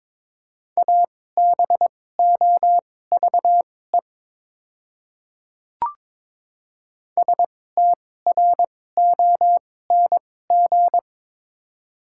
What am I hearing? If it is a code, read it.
Morse code, heard as a keyed tone, about 21 wpm, spaced as for 16 wpm, 700 Hz. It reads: ABOVE STRONG
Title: 2x Character Spacing - Top 500 Words - 22wpm